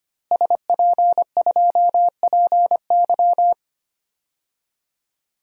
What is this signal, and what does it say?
Morse code, heard as a keyed tone, about 25 wpm, 700 Hz. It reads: SP2PY
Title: DXCC-100 Call Signs - 5 Characters - Repeated 2x - 25wpm